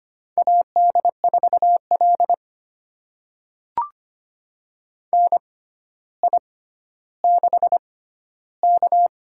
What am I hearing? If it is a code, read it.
Morse code, heard as a keyed tone, about 25 wpm, spaced as for 13 wpm, 700 Hz. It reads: AD4L NI6K
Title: USA Call Signs - 4 Characters - 6x Character Spacing - 25wpm